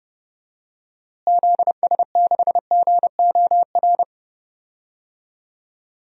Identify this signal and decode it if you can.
Morse code, heard as a keyed tone, about 30 wpm, 700 Hz. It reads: ZS6GOR
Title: DXCC-100 Call Signs - 6 Characters - Repeated 3x - 30wpm